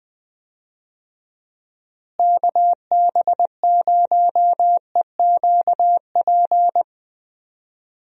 Morse code, heard as a keyed tone, about 20 wpm, 700 Hz. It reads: KB0EQP
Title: USA Call Signs - 6 Characters - Repeated 2x - 20wpm